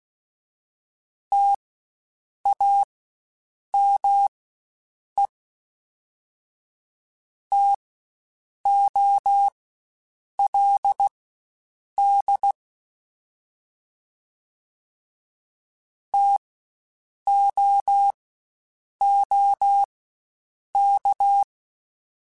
Morse code, heard as a keyed tone, about 16 wpm, spaced as for 7 wpm, 800 Hz. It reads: TAME TOLD TOOK